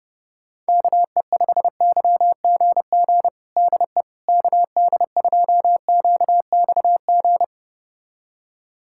Morse code, heard as a keyed tone, about 32 wpm, 700 Hz. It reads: KE5YGG DE KD2QXG
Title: USA Callsign DE Callsign - 6 Characters - 30wpm